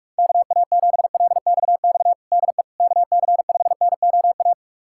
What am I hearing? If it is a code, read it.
Morse code, heard as a keyed tone, about 46 wpm, 700 Hz. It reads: KA7LXX DE KC5NOA
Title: USA Callsign DE Callsign 45wpm